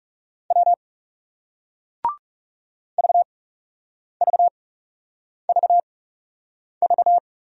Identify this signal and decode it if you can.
Morse code, heard as a keyed tone, about 39 wpm, spaced as for 8 wpm, 700 Hz. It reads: W VVVV